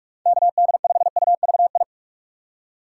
Morse code, heard as a keyed tone, about 44 wpm, 700 Hz. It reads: KD5UVI
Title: USA Call Signs - 6 Characters - 6x Character Spacing - 45wpm